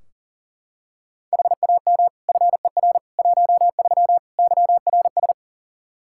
Morse code, heard as a keyed tone, about 40 wpm, 700 Hz. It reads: HAM FER 13 YRS
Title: Practice QSO - Exchange 3 - More Info - 40wpm